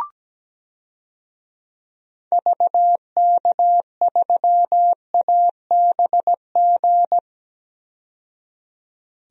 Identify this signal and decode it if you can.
Morse code, heard as a keyed tone, about 17 wpm, 700 Hz. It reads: VK3ABG
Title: DXCC-100 Call Signs - 6 Characters - 17wpm